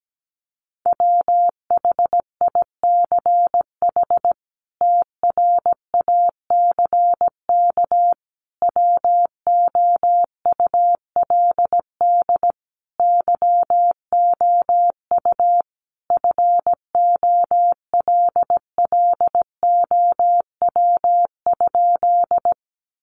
Morse code, heard as a keyed tone, about 17 wpm, 700 Hz. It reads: WHICH TRACK WOULD YOU FOLLOW?
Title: Sentences from Top 1100 Words 17wpm